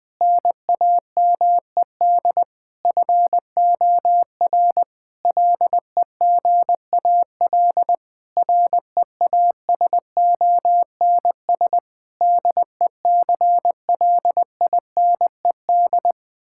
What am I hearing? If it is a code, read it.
Morse code, heard as a keyed tone, about 20 wpm, 700 Hz. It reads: NAMED FOR LEGAL REASONS DECLINED
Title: Sentences with 8-Word Phrases - Spoken Beginning - 20wpm